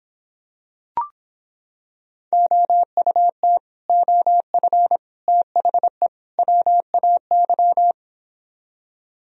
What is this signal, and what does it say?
Morse code, heard as a keyed tone, about 26 wpm, 700 Hz. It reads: OUT OF THE WAY